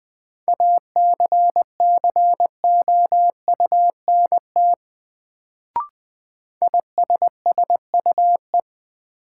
Morse code, heard as a keyed tone, about 20 wpm, 700 Hz. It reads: ACCOUNT ISSUE
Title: Sentences from Top 900 Words - New Word Drill - 20wpm